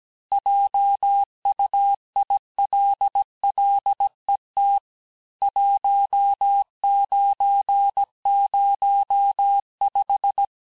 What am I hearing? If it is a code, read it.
Morse code, heard as a keyed tone, about 17 wpm, 800 Hz. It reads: JUILLET 1905
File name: texte clair vitesse 16 puis 21 mpm